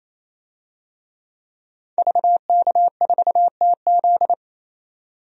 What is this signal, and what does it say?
Morse code, heard as a keyed tone, about 28 wpm, 700 Hz. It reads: VK4TZ
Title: DXCC-100 Call Signs - 5 Characters - 28wpm